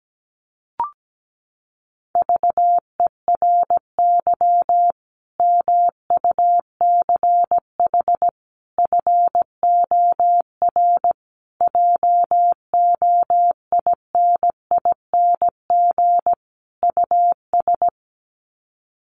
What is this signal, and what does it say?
Morse code, heard as a keyed tone, about 17 wpm, 700 Hz. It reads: VERY MUCH FOR JOINING US
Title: Sets of 5 Words 17wpm